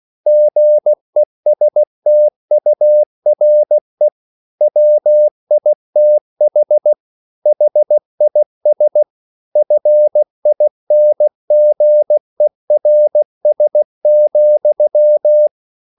Morse code, heard as a keyed tone, about 16 wpm, 600 Hz. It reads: GESTURE WITH HIS FINGERS,